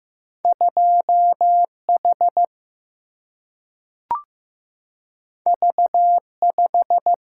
Morse code, heard as a keyed tone, about 15 wpm, 700 Hz. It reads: 2 H V 5
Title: Single-Letter Single-Number 15wpm